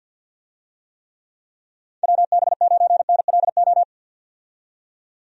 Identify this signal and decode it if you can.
Morse code, heard as a keyed tone, about 50 wpm, 700 Hz. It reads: WB9NLO
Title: USA Call Signs - 6 Characters - 50wpm